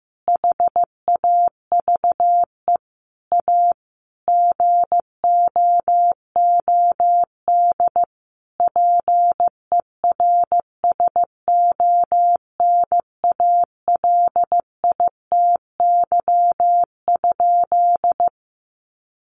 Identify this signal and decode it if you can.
Morse code, heard as a keyed tone, about 15 wpm, 700 Hz. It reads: HAVE A GOOD PERSONALITY?
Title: Sentences from Top 1400 Words 15wpm